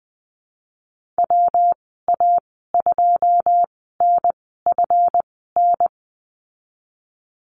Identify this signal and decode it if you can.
Morse code, heard as a keyed tone, about 20 wpm, spaced as for 15 wpm, 700 Hz. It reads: WA2NFN
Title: 154 Lesson 25g - Letter p - Call Signs 20wpm